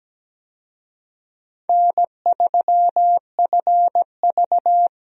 Morse code, heard as a keyed tone, about 17 wpm, 700 Hz. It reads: N3FV
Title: USA Call Signs - 4 Characters - Repeated 3x - 17wpm